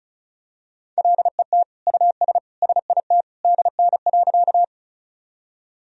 Morse code, heard as a keyed tone, about 36 wpm, 700 Hz. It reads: LET US SIT DN.
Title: Sentences from Top 500 Words and Abbreviations 35wpm